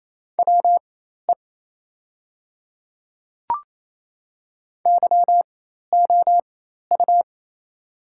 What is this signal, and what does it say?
Morse code, heard as a keyed tone, about 28 wpm, spaced as for 13 wpm, 700 Hz. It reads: WE YOU
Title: 4x Character Spacing - Top 100 Words - 28wpm